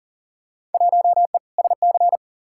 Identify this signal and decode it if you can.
Morse code, heard as a keyed tone, about 38 wpm, 700 Hz. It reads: 1E SC
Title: Field Day Exchanges - 40wpm